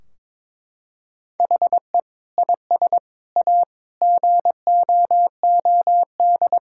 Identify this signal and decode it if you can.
Morse code, heard as a keyed tone, about 21 wpm, 700 Hz. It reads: HE IS A GOOD
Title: Sentences from Top 1600 Words 22wpm